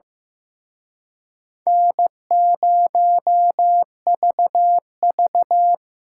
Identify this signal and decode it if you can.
Morse code, heard as a keyed tone, about 18 wpm, 700 Hz. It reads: N0VV
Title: Speed Racing - USA 1x2 Call Signs 15wpm